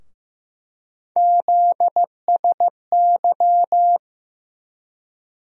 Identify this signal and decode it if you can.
Morse code, heard as a keyed tone, about 15 wpm, 700 Hz. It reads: ZSY
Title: USA Call Sign Postfixes - 3 Letters 15wpm